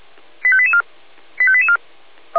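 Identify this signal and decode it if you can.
Five-tone selective call (ZVEI), heard as quick runs of five tones, tones of about 80 ms. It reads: 85884 85884
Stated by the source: ZVEI five-tone call received via SDR.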